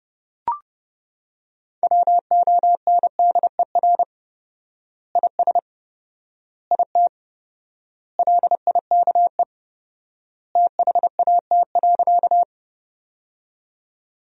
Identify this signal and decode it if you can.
Morse code, heard as a keyed tone, about 29 wpm, 700 Hz. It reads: WONDER IS IT LIKE THAT.
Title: 4x Word-Spacing - Sentences from Top 500 Words - 30wpm